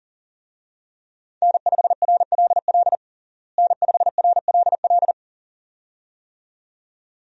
Morse code, heard as a keyed tone, about 40 wpm, 700 Hz. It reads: N5RLL N5RLL
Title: USA Call Signs - 5 Characters - Repeated 2x - 40wpm